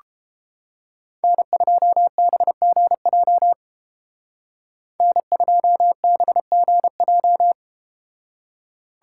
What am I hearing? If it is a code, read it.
Morse code, heard as a keyed tone, about 31 wpm, 700 Hz. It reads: N2BGJ N2BGJ